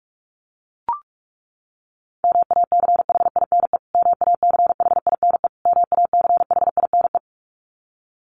Morse code, heard as a keyed tone, about 44 wpm, 700 Hz. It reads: MACHINE MACHINE MACHINE
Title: Top 500 Words Repeated 45wpm